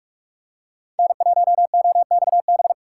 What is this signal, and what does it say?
Morse code, heard as a keyed tone, about 46 wpm, 700 Hz. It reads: N1OXB
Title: USA Call Signs - 5 Characters - 3x Character Spacing - 45wpm